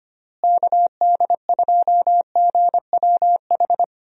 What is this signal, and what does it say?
Morse code, heard as a keyed tone, about 25 wpm, 700 Hz. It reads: KD2GWH
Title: USA Call Signs - 6 Characters - Repeated 3x - 25wpm